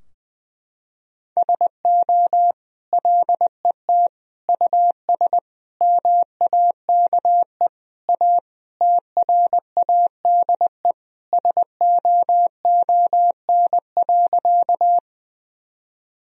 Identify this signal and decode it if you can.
Morse code, heard as a keyed tone, about 20 wpm, 700 Hz. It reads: SO LET US MAKE A TRADE SOON.
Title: Sentences from Top 1000 Words - Review Part 4 - 20wpm